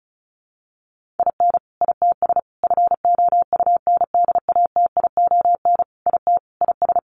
Morse code, heard as a keyed tone, about 34 wpm, 700 Hz. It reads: IN ITS FOUNDATION IT IS
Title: Sentences from Top 1200 Words 35wpm